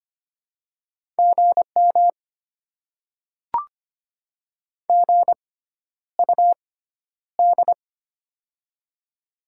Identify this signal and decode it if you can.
Morse code, heard as a keyed tone, about 25 wpm, spaced as for 11 wpm, 700 Hz. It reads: GM GUD
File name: Top 20 QSO Elements - 6x Character Spacing - 25wpm